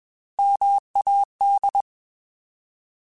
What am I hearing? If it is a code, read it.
Morse code, heard as a keyed tone, about 21 wpm, 800 Hz. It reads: MAD
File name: Alphabetical / Alphabetical mp3s / Ms @20